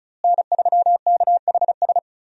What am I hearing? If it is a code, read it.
Morse code, heard as a keyed tone, about 36 wpm, 700 Hz. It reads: N3KHS
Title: USA Call Signs - 5 Characters - Repeated 3x - 35wpm